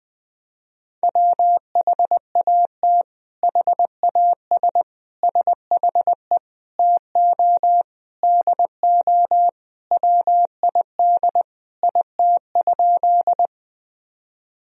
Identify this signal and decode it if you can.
Morse code, heard as a keyed tone, about 20 wpm, 700 Hz. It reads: WHAT HAS SHE TO DO WID IT?
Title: Sentences from Top 100 Words and Abbreviations 20wpm